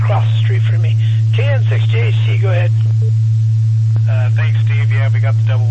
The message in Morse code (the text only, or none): none